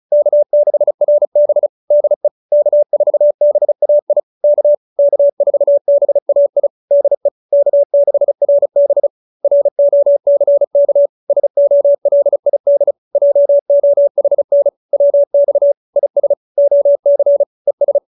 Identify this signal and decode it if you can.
Morse code, heard as a keyed tone, about 36 wpm, 570 Hz. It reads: K6RB DE K4BAI K K4BAI DE K6RB ROCK SOLID JOHN WX IS OC ES